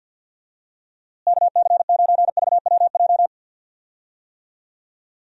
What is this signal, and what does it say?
Morse code, heard as a keyed tone, about 51 wpm, 700 Hz. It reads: KC9VWJ
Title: USA Call Signs - 6 Characters - Repeated 3x - 50wpm